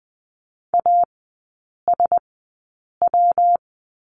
Morse code, heard as a keyed tone, about 20 wpm, spaced as for 8 wpm, 700 Hz. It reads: ASW
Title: 172 Lesson 28e - Number 9 - Three Characters 20wpm